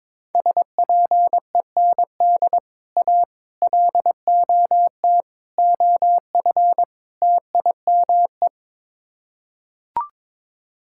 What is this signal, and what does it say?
Morse code, heard as a keyed tone, about 22 wpm, 700 Hz. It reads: SPEND A LOT OF TIME
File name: practice / 22wpm / Build A Phrase - 5-Word Phrases - 22wpm